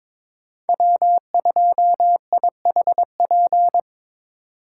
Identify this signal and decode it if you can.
Morse code, heard as a keyed tone, about 21 wpm, 700 Hz. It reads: W2IHP
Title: USA Call Signs - 5 Characters - 4x Character Spacing - 22wpm